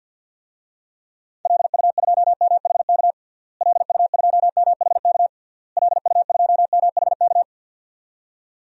Morse code, heard as a keyed tone, about 49 wpm, 700 Hz. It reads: LU2MHK LU2MHK LU2MHK
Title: DXCC-100 Call Signs - 6 Characters - Repeated 3x - 50wpm